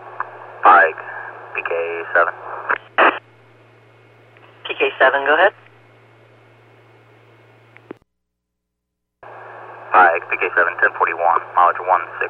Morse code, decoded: none

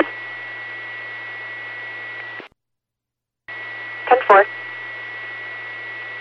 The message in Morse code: none

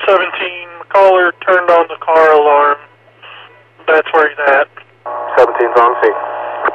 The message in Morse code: none

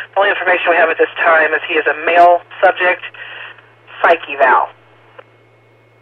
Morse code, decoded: none

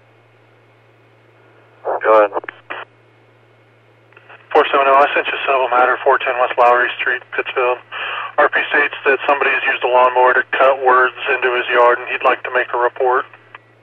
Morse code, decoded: none

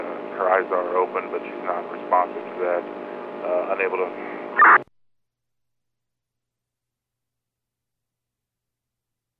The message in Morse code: none

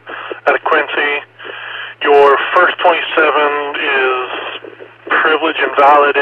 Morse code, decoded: none

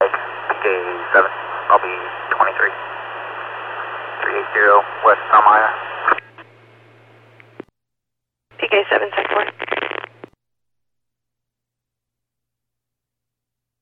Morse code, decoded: none

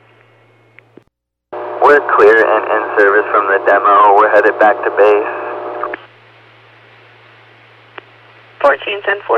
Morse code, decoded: none

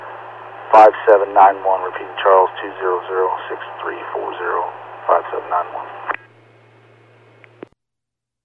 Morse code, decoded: none

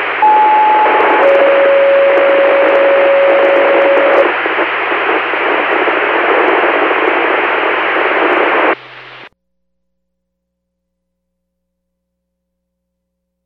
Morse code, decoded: none

